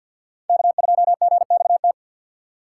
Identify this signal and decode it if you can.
Morse code, heard as a keyed tone, about 52 wpm, 700 Hz. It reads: K2GXT
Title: USA Call Signs - 6x Character Spacing - 50wpm